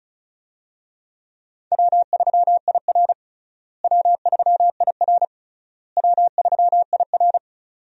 Morse code, heard as a keyed tone, about 36 wpm, 700 Hz. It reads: W3IR W3IR W3IR
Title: USA Call Signs - 4 Characters - Repeated 3x - 35wpm